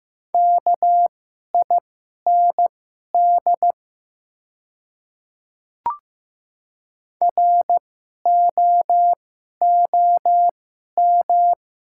Morse code, heard as a keyed tone, about 15 wpm, spaced as for 11 wpm, 700 Hz. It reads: KIND ROOM
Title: 2x Character Spacing - Top 300 Words - 15wpm